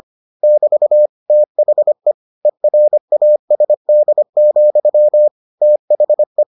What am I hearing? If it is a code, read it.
Morse code, heard as a keyed tone, about 25 wpm, 600 Hz. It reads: <BT> THE ERASD, THE